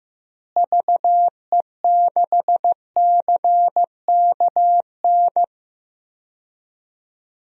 Morse code, heard as a keyed tone, about 15 wpm, 700 Hz. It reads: VE6CKN
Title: DXCC-100 Call Signs - 6 Characters - Repeated 3x - 15wpm